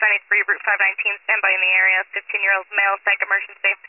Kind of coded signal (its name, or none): Emergency Alert System (SAME) tones